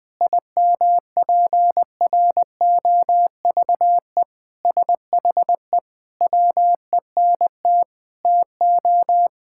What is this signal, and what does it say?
Morse code, heard as a keyed tone, about 20 wpm, 700 Hz. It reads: IMPROVE SHE WENT TO
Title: Sentences with 4-Word Phrases - 20wpm